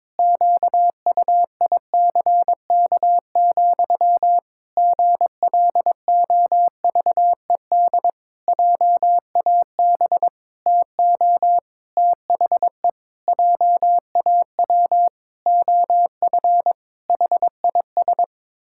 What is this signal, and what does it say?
Morse code, heard as a keyed tone, about 22 wpm, 700 Hz. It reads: QUICK, GLOVED JAB TO THE JAW OF HIS
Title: Pangram Challenge v2 - 22wpm